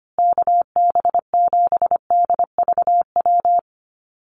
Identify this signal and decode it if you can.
Morse code, heard as a keyed tone, about 25 wpm, 700 Hz. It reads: KB7DVW
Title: USA Call Signs Repeated 3x 25wpm